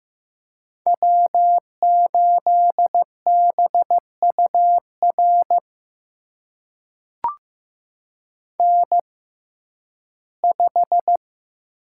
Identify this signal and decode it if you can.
Morse code, heard as a keyed tone, about 15 wpm, spaced as for 11 wpm, 700 Hz. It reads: W8BUR N5
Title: USA Call Signs - 6x Character Spacing - 15wpm